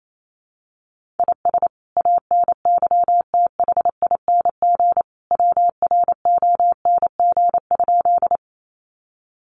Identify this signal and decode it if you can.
Morse code, heard as a keyed tone, about 28 wpm, 700 Hz. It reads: IS ANYTHING WRONG?
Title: Sentences from Top 1000 Words 28wpm